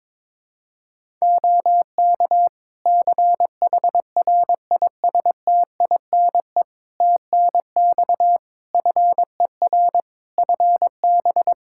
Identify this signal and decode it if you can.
Morse code, heard as a keyed tone, about 22 wpm, 700 Hz. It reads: OK CHRISTINE TNX FER FB
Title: Practice QSO - Exchange 4 - Ending - 22wpm